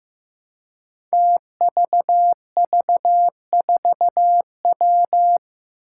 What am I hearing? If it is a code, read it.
Morse code, heard as a keyed tone, about 15 wpm, 700 Hz. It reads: TVV4W
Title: Five Challenging Letter-Number Combinations 15wpm